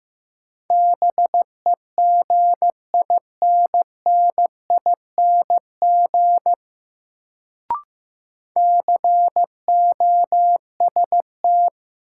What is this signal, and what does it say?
Morse code, heard as a keyed tone, about 16 wpm, 700 Hz. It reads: BEGINNING COST
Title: Sets of 3 Words - New Word Drill - 15wpm